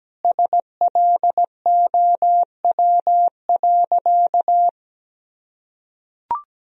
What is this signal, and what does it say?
Morse code, heard as a keyed tone, about 17 wpm, 700 Hz. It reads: SLOW.
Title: Sentences from Top 1000 Words - Review Part 2 - 17wpm